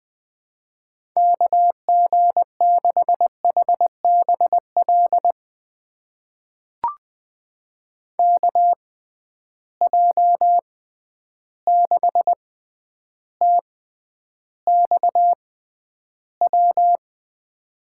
Morse code, heard as a keyed tone, about 20 wpm, spaced as for 10 wpm, 700 Hz. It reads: KG6HBL KJ6TXW
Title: USA Call Signs - 6 Characters - 6x Character Spacing - 20wpm